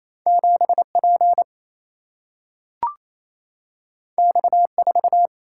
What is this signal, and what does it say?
Morse code, heard as a keyed tone, about 28 wpm, 700 Hz. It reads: 7 P X 4
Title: Single-Letter Single-Number 28wpm